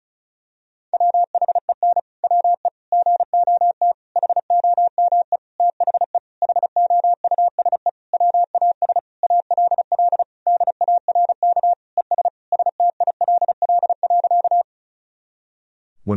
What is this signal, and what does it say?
Morse code, heard as a keyed tone, about 36 wpm, 700 Hz. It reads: WHEN WE GOT HOME THE HOUSE WAS ALL DARK ES STILL.
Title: Sentences from Top 500 Words and Abbreviations 35wpm